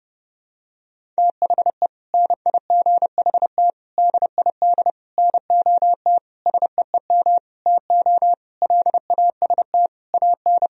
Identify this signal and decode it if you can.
Morse code, heard as a keyed tone, about 30 wpm, 700 Hz. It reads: THE NIGHT DID NOT SEEM TO LAST AN
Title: Sentences from Top 1000 Words - Review Part 1 - 30wpm